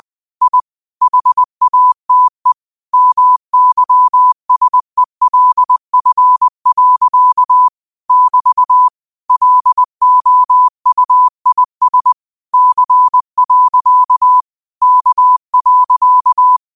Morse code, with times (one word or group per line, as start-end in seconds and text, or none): 0.4-0.6 I
1.0-2.5 HATE
2.9-7.7 MYSELF.
8.1-8.9 <BT>
9.3-12.1 LOUIS
12.5-14.4 C.
14.8-16.6 K.